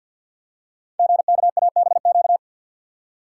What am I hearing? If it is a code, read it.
Morse code, heard as a keyed tone, about 52 wpm, 700 Hz. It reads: GKABQ
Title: Code Groups - 5 Letters - 50wpm